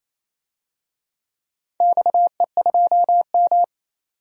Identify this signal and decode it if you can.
Morse code, heard as a keyed tone, about 28 wpm, 700 Hz. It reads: XE2M
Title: DXCC-100 Call Signs - 4 Characters - 28wpm